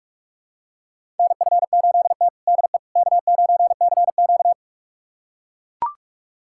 Morse code, heard as a keyed tone, about 42 wpm, 700 Hz. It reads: NF8T DE K9CQ
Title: USA Callsign DE Callsign - 4 Characters - 45wpm